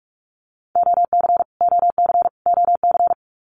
Morse code, heard as a keyed tone, about 44 wpm, 700 Hz. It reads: OC OC OC
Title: Abbreviations Repeated 3x - 45wpm